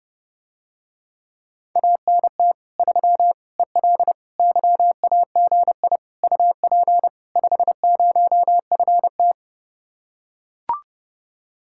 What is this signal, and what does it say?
Morse code, heard as a keyed tone, about 30 wpm, 700 Hz. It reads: ANT 3 EL YAGI UP 50FT